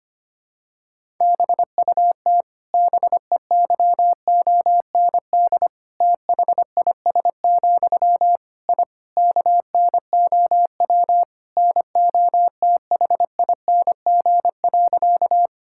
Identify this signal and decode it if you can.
Morse code, heard as a keyed tone, about 25 wpm, 700 Hz. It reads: BUT BEYOND THIS, I KNOW NOTHING.
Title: Sentences from Top 1000 Words - Review Part 6 - 25wpm